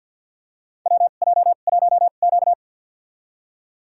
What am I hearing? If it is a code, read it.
Morse code, heard as a keyed tone, about 47 wpm, spaced as for 35 wpm, 700 Hz. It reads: WJ1Q
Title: USA Call Signs - 4 Characters - 2x Character Spacing - 50wpm